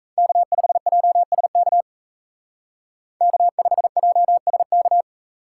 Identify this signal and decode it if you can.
Morse code, heard as a keyed tone, about 34 wpm, 700 Hz. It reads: K5JSK K5JSK
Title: Speed Racing - USA 1x3 Call Signs 28wpm